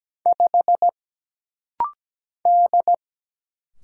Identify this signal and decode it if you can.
Morse code, heard as a keyed tone, about 17 wpm, 700 Hz. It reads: 5 D
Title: Single Letter-Number - Rapid-Fire 17wpm